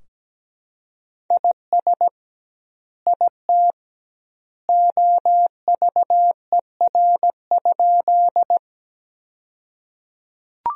Morse code, heard as a keyed tone, about 17 wpm, 700 Hz. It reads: IS IT OVER?
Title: 2x Word-Spacing - Sentences from Top 100 Words - 17wpm